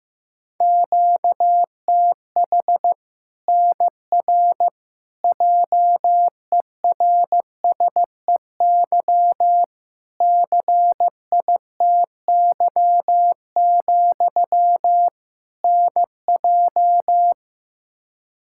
Morse code, heard as a keyed tone, about 16 wpm, 700 Hz. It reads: QTH NR JERSEY CITY, NJ
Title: K1USN SST QSO Practice 15wpm